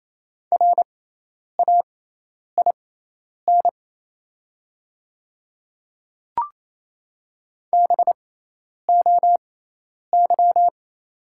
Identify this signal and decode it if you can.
Morse code, heard as a keyed tone, about 28 wpm, spaced as for 10 wpm, 700 Hz. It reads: RAIN BOY
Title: 6x Character Spacing - Top 300 Words - 28wpm